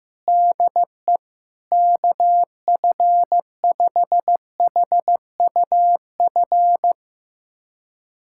Morse code, heard as a keyed tone, about 15 wpm, 700 Hz. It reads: DE KF5HUF
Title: USA Callsign DE Callsign 15wpm